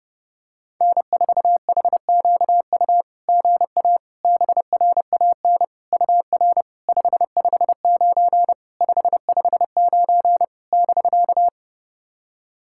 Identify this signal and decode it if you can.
Morse code, heard as a keyed tone, about 28 wpm, 690 Hz. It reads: N4HQU GA BRAN UR 559 559 <BK>
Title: SOTA QSO Practice 30wpm